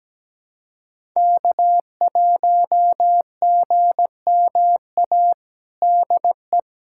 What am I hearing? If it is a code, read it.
Morse code, heard as a keyed tone, about 17 wpm, 700 Hz. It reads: K1GMA DE
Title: USA Callsign DE Callsign - 5 Characters - 17wpm